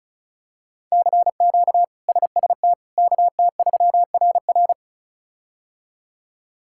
Morse code, heard as a keyed tone, about 36 wpm, 700 Hz. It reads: CQ SST KT3RR
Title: K1USN SST QSO Practice - 35wpm